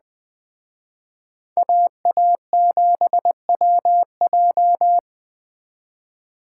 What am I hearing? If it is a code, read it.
Morse code, heard as a keyed tone, about 20 wpm, 700 Hz. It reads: AA7WJ
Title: Speed Racing - USA 2x2 Call Signs 17wpm